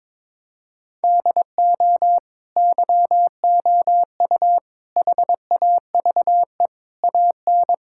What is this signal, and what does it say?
Morse code, heard as a keyed tone, about 22 wpm, 700 Hz. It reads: DO YOU HAVE AN
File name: Sets of 9 Words - 22wpm